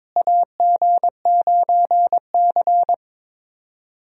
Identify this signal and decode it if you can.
Morse code, heard as a keyed tone, about 22 wpm, 700 Hz. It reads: AG9C
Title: USA Call Signs - 4 Characters - Repeated 2x - 22wpm